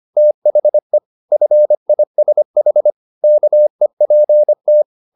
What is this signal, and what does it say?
Morse code, heard as a keyed tone, about 25 wpm, 600 Hz. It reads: THE FISH KEPT